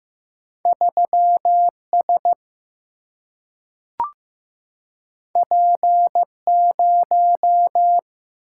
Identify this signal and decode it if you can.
Morse code, heard as a keyed tone, about 15 wpm, 700 Hz. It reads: 3 S P 0
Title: Single-Letter Single-Number 15wpm